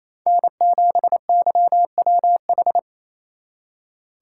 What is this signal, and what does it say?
Morse code, heard as a keyed tone, about 36 wpm, 700 Hz. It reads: N7YWH